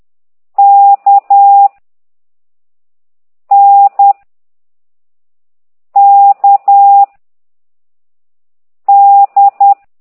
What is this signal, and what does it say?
Morse code, heard as a keyed tone, about 10 wpm, 800 Hz. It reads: K N K D
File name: ct2idw / exerc4